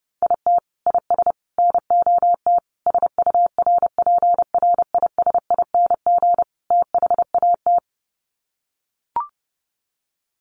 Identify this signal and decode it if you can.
Morse code, heard as a keyed tone, about 30 wpm, 700 Hz. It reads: IT IS NOT SURPRISING THAT
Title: Sets of 5 Words 30wpm